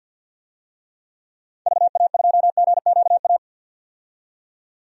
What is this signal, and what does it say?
Morse code, heard as a keyed tone, about 52 wpm, 700 Hz. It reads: VA2GQA
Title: DXCC-100 Call Signs - 6 Characters - 50wpm